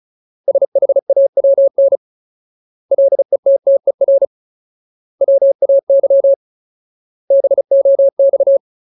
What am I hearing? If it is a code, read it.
Morse code, heard as a keyed tone, about 36 wpm, 550 Hz. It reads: SHAWN LETTER WAY BOX
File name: Ultimate Word List / 35WPM Ultimate